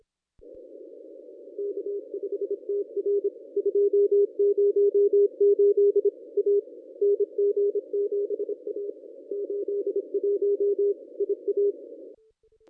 Morse code, heard as a keyed tone, about 26 wpm, 400 Hz. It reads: K5TR 208 A NG7A 81 IA
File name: contests / ss-cw-k5tr-op-n5rz -1200